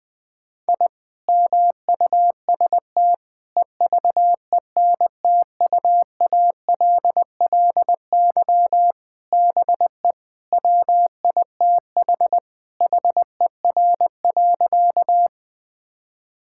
Morse code, heard as a keyed tone, about 20 wpm, 700 Hz. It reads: I MUST EVENTUALLY BE WITH HER.
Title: Sentences from Top 1000 Words - Review Part 6 - 20wpm